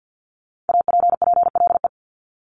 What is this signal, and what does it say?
Morse code, heard as a keyed tone, about 50 wpm, 700 Hz. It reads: APPLE
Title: Multi-letter Phonemes 50wpm